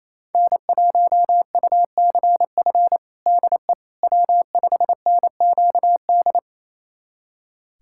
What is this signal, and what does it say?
Morse code, heard as a keyed tone, about 27 wpm, 700 Hz. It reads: N1UCF DE W5NQD
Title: USA Callsign DE Callsign - 5 Characters - 28wpm